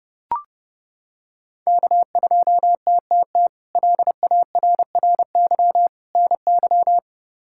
Morse code, heard as a keyed tone, about 32 wpm, 700 Hz. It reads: K2TTT LARRY NY
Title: K1USN SST QSO Practice - 30wpm